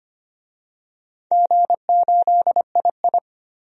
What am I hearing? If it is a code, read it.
Morse code, heard as a keyed tone, about 25 wpm, 700 Hz. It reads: G8II
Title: DXCC-100 Call Signs - 4 Characters - 25wpm